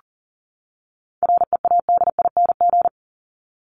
Morse code, heard as a keyed tone, about 40 wpm, 700 Hz. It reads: READING
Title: ING Suffix - Top 100 Words with Base Words 40wpm